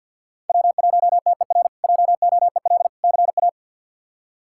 Morse code, heard as a keyed tone, about 51 wpm, 700 Hz. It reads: W1TER JOEL CA
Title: K1USN SST QSO Practice - 50wpm